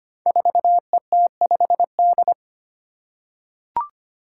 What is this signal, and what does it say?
Morse code, heard as a keyed tone, about 25 wpm, 700 Hz. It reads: 4ET5D
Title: Five Challenging Letter-Number Combinations 25wpm